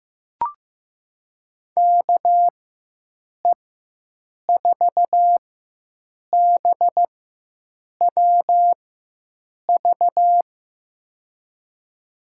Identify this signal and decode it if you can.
Morse code, heard as a keyed tone, about 15 wpm, spaced as for 7 wpm, 700 Hz. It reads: KE4BWV
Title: USA Call Signs - 6 Characters - 4x Character Spacing - 15wpm